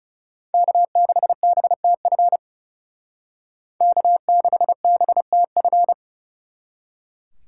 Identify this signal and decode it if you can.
Morse code, heard as a keyed tone, about 37 wpm, 700 Hz. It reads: K6BTF K6BTF
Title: Speed Racing - USA 1x3 Call Signs 30wpm